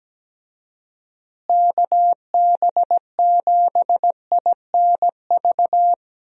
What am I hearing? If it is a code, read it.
Morse code, heard as a keyed tone, about 17 wpm, 700 Hz. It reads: KB7INV
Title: USA Call Signs - 6 Characters - Repeated 2x - 17wpm